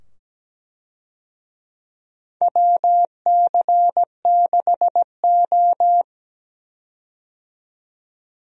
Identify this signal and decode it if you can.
Morse code, heard as a keyed tone, about 17 wpm, 700 Hz. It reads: WC6O
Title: USA Call Signs - 4 Characters - 17wpm